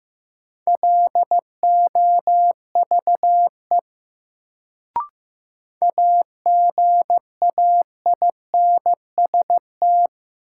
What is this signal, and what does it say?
Morse code, heard as a keyed tone, about 15 wpm, 700 Hz. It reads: LOVE AGAINST
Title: Sentences from Top 400 Words - New Word Drill - 15wpm